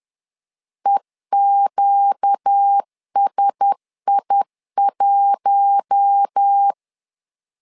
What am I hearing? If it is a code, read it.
Morse code, heard as a keyed tone, about 10 wpm, 800 Hz. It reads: EQSI1
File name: LEZIONI / 4 Q1S 10wpm 2133